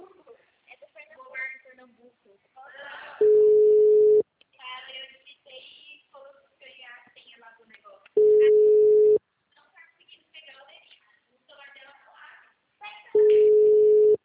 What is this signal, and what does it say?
Ringback tone (Europe, 425 Hz), with speech.